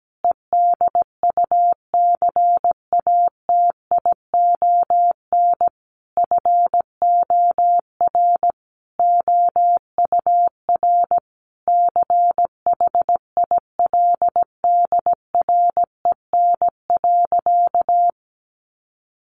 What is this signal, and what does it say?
Morse code, heard as a keyed tone, about 17 wpm, 700 Hz. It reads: EDUCATION FOR OUR CHILDREN.
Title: Sentences from Top 1400 Words 17wpm